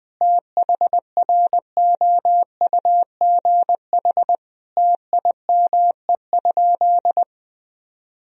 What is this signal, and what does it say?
Morse code, heard as a keyed tone, about 20 wpm, 700 Hz. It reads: THROUGH TIME?
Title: Sentences from Top 1000 Words - Review Part 2 - 20wpm